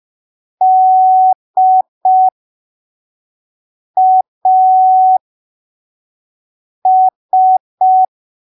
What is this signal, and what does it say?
Morse code, heard as a keyed tone, about 5 wpm, spaced as for 3 wpm, 750 Hz. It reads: DAS
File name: arquivos / texto / texto 23 05ppm